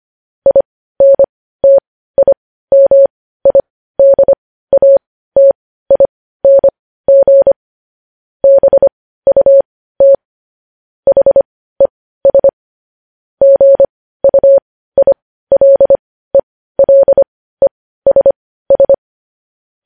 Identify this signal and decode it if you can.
Morse code, heard as a keyed tone, about 25 wpm, spaced as for 15 wpm, 550 Hz. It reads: INTIMIDATING BUT HES GUILELESS
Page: News Headlines in Morse Code at 15 WPM